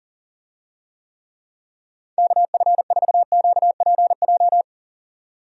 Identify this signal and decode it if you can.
Morse code, heard as a keyed tone, about 40 wpm, 700 Hz. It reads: KF4QPJ
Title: USA Call Signs - 6 Characters - 40wpm